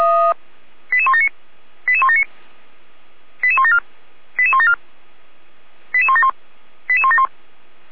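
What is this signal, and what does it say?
Five-tone selective call (ZVEI), heard as quick runs of five tones, tones of about 70 ms. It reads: 88179 88179 88174 88174 88171 88171
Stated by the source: ZVEI five-tone selective call received by SDR.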